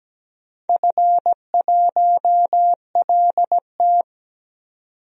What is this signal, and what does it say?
Morse code, heard as a keyed tone, about 17 wpm, 700 Hz. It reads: F1LT
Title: DXCC-100 Call Signs - 4 Characters - 17wpm